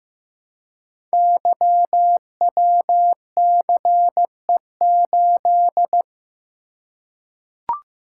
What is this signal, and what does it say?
Morse code, heard as a keyed tone, about 15 wpm, 700 Hz. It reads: YWCE8
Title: Code Groups - 5 Letter:Numbers - 15wpm